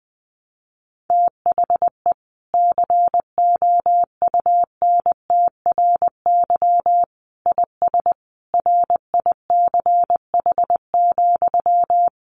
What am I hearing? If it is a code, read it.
Morse code, heard as a keyed tone, about 20 wpm, 700 Hz. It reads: THE COUNTRY IS RICH,
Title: Sentences from Top 1100 Words 20wpm